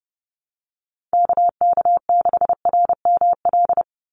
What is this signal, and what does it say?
Morse code, heard as a keyed tone, about 30 wpm, 700 Hz. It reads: KK6RML
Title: USA Call Signs 30wpm